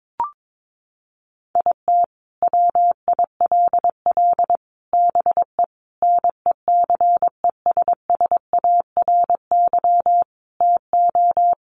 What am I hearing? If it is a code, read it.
Morse code, heard as a keyed tone, about 22 wpm, 700 Hz. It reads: IT WILL BE NECESSARY TO
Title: Sentences from Top 800 Words 22wpm